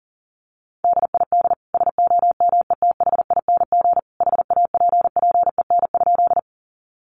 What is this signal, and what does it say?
Morse code, heard as a keyed tone, about 40 wpm, 700 Hz. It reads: DID SOMETHING HAPPEN?
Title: Sentences from Top 600 Words 40wpm